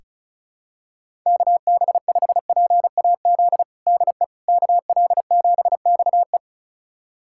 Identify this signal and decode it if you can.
Morse code, heard as a keyed tone, about 36 wpm, 700 Hz. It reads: KB5PAZ DE KL7XE
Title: USA Callsign DE Callsign 35wpm